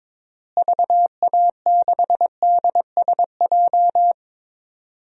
Morse code, heard as a keyed tone, about 22 wpm, 700 Hz. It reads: VA6DSJ
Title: DXCC-100 Call Signs - 6 Characters - Repeated 2x - 22wpm